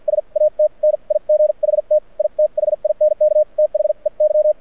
Morse code, heard as a keyed tone, about 54 wpm, spaced as for 42 wpm, 610 Hz. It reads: SAT NIGHT I THINK THEY
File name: archived / CRSnet